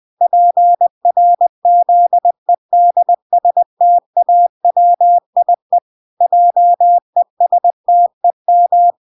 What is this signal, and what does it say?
Morse code, heard as a keyed tone, about 20 wpm, 700 Hz. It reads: PRZEDSTAWIE JESTEM
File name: 20wpm